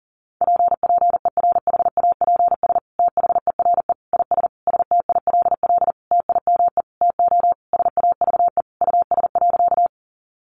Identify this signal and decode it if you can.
Morse code, heard as a keyed tone, about 40 wpm, 700 Hz. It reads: PPERHAPS THERE IS STILL TIME TO SAVE US.
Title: Sentences from Top 1100 Words 40wpm